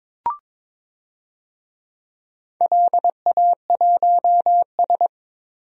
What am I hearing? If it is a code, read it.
Morse code, heard as a keyed tone, about 22 wpm, 700 Hz. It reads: LA1S
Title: DXCC-100 Call Signs - 4 Characters - Repeated 2x - 22wpm